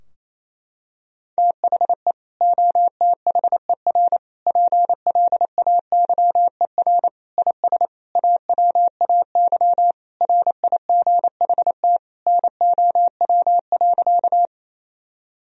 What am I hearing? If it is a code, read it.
Morse code, heard as a keyed tone, about 27 wpm, 700 Hz. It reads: THE OTHER PLAYER IS AWAY RIGHT NOW.
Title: Sentences from Top 1000 Words - Review Part 4 - 28wpm